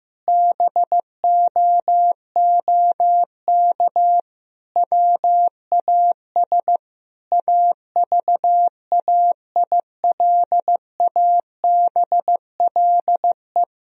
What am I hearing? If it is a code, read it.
Morse code, heard as a keyed tone, about 15 wpm, 700 Hz. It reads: BOOK WAS AVAILABLE
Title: Sentences from Top 1000 Words - Review Part 3 - 15wpm